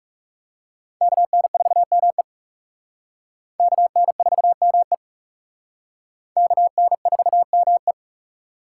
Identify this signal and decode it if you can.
Morse code, heard as a keyed tone, about 37 wpm, 700 Hz. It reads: KN4ME KN4ME KN4ME
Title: Speed Racing - USA 2x2 Call Signs 30wpm